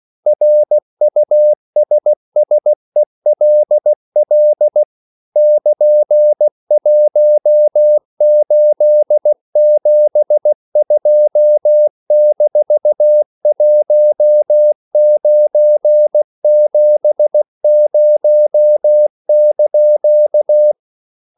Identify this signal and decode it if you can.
Morse code, heard as a keyed tone, about 16 wpm, 600 Hz. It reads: RUSSELL <KN>1872-1970)
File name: CW / Practice / EnglishQuotations0116